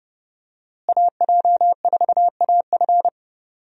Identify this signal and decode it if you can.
Morse code, heard as a keyed tone, about 30 wpm, 700 Hz. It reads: AJ4AF
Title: USA Call Signs - 5 Characters - Repeated 3x - 30wpm